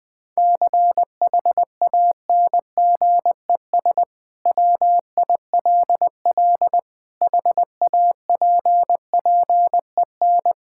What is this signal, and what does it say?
Morse code, heard as a keyed tone, about 20 wpm, 700 Hz. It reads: CHANGES WILL HAPPEN
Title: Sentences with 5-Word Phrases - Spoken Beginning - 20wpm